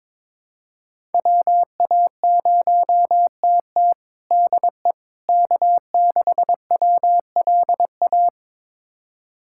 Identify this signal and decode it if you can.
Morse code, heard as a keyed tone, about 22 wpm, 700 Hz. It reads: WA0TT DE K6WLA
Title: USA Callsign DE Callsign - 5 Characters - 22wpm